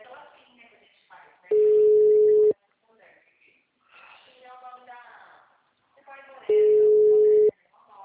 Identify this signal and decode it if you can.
Ringback tone (Europe, 425 Hz), with speech.